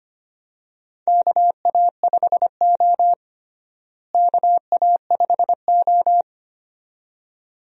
Morse code, heard as a keyed tone, about 25 wpm, 700 Hz. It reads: KA5O KA5O
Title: USA Call Signs - 4 Characters - Repeated 2x - 25wpm